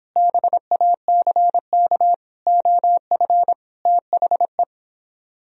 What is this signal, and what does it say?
Morse code, heard as a keyed tone, about 29 wpm, 700 Hz. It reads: BACK OF THE